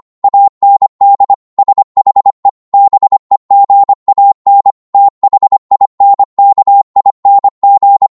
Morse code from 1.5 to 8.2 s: SHE BEGAN THINKING